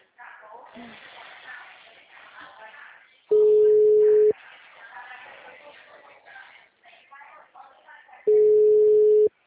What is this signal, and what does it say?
Ringback tone (Europe, 425 Hz), with speech.